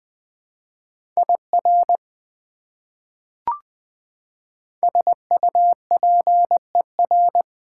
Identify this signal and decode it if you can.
Morse code, heard as a keyed tone, about 20 wpm, 700 Hz. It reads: IR SUPER
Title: Prefixes and Suffixes 20wpm